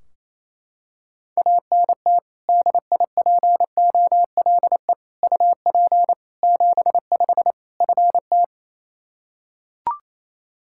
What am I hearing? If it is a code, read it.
Morse code, heard as a keyed tone, about 28 wpm, 700 Hz. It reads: ANT DIPOLE UP 75 FT
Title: Practice QSO - Exchange 2 - Info - 28wpm